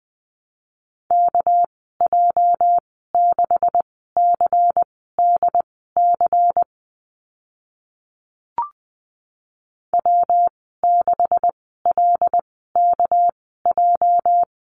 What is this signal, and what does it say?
Morse code, heard as a keyed tone, about 20 wpm, spaced as for 14 wpm, 700 Hz. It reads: KJ6CDC W6LKJ
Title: 213 Lesson 34g - Number 8 - Call Signs 20wpm